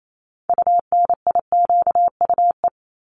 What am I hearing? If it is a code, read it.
Morse code, heard as a keyed tone, about 28 wpm, 700 Hz. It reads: UNIQUE
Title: Top 751-1000 Words 28wpm